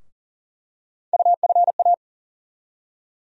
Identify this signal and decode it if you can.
Morse code, heard as a keyed tone, about 40 wpm, 700 Hz. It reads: UFA
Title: USA Call Sign Postfixes - 3 Letters 40wpm